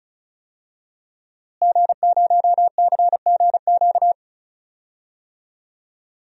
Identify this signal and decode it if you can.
Morse code, heard as a keyed tone, about 36 wpm, 690 Hz. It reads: G0CGQ
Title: DXCC-100 Call Signs - 5 Characters - Repeated 3x - 35wpm